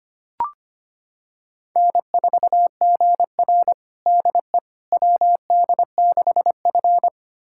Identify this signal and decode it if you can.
Morse code, heard as a keyed tone, about 25 wpm, 700 Hz. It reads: N4GR DE WD6F
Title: USA Callsign DE Callsign - 4 Characters - 25wpm